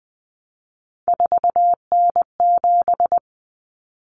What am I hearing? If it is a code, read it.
Morse code, heard as a keyed tone, about 20 wpm, 700 Hz. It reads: 4N7
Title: Common Cut Numbers - 3-Digit 20wpm